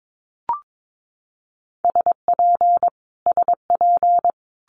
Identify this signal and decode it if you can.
Morse code, heard as a keyed tone, about 22 wpm, 700 Hz. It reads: SP SP
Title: Multi-letter Phonemes Repeated 3x 22wpm